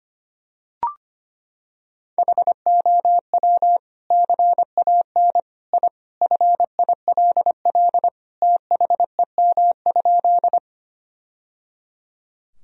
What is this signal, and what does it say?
Morse code, heard as a keyed tone, about 25 wpm, 700 Hz. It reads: HOW CAN I FILL THEM?
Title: Sentences from Top 1000 Words - Review Part 3 - 25wpm